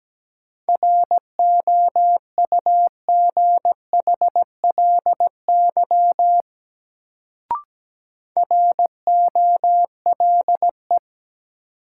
Morse code, heard as a keyed tone, about 17 wpm, 700 Hz. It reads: ROUGHLY ROLE
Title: Sentences from Top 1500 Words - New Word Drill Alt - 17wpm